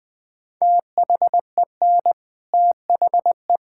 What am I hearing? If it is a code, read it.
Morse code, heard as a keyed tone, about 21 wpm, 700 Hz. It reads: THEN THE
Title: Sentences from Top 1600 Words 20wpm